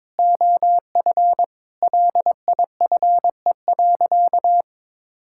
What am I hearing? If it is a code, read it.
Morse code, heard as a keyed tone, about 21 wpm, 700 Hz. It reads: OF LIFE.
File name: practice / 22wpm / Sentences with 8-Word Phrases - 22wpm